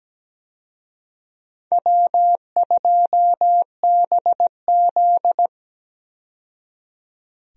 Morse code, heard as a keyed tone, about 17 wpm, 700 Hz. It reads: W2BZ
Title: USA Call Signs - 4 Characters - 17wpm